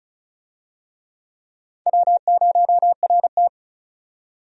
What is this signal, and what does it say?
Morse code, heard as a keyed tone, about 34 wpm, 700 Hz. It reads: W0RT
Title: USA Call Signs - 4 Characters - 35wpm